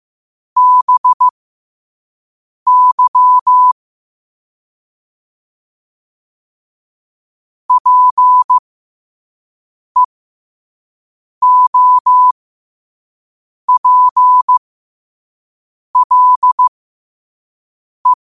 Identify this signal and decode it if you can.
Morse code, heard as a keyed tone, about 15 wpm, spaced as for 5 wpm, 1000 Hz. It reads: BY PEOPLE
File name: qotdfiles / 5wpm